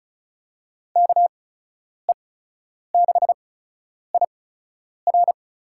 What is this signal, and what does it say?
Morse code, heard as a keyed tone, about 36 wpm, spaced as for 10 wpm, 700 Hz. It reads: KE6IR
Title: USA Call Signs - 5 Characters - 8x Character Spacing - 35wpm